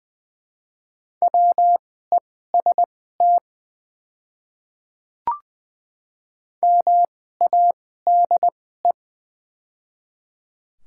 Morse code, heard as a keyed tone, about 20 wpm, spaced as for 14 wpm, 700 Hz. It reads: WEST MADE
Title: 2x Character Spacing - Top 500 Words - 20wpm